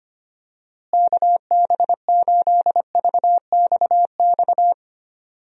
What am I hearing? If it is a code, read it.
Morse code, heard as a keyed tone, about 25 wpm, 700 Hz. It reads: KB8VXX
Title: USA Call Signs - 6 Characters - Repeated 2x - 25wpm